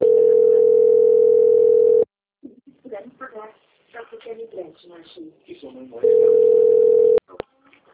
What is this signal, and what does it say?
Ringback tone (North America), 440 + 480 Hz, with speech.
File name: out- - - - - .32344